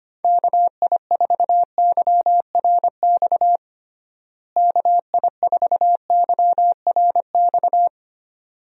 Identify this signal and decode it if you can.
Morse code, heard as a keyed tone, about 25 wpm, 700 Hz. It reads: KI4YRX KI4YRX
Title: USA Call Signs - 6 Characters - Repeated 2x - 25wpm